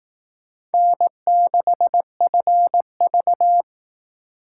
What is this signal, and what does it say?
Morse code, heard as a keyed tone, about 18 wpm, 700 Hz. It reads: N6FV
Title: Speed Racing - USA 1x2 Call Signs 15wpm